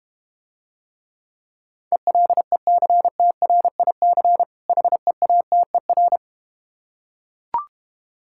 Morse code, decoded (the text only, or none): ELECTRIC HEATER